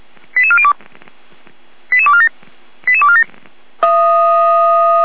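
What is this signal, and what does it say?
Five-tone selective call (ZVEI), heard as quick runs of five tones, tones of about 70 ms. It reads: 88402 88247 88247
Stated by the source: ZVEI five-tone call received on SDR.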